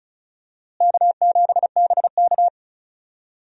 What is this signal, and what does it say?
Morse code, heard as a keyed tone, about 34 wpm, 700 Hz. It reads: K7BK
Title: USA Call Signs - 4 Characters - Repeated 3x - 35wpm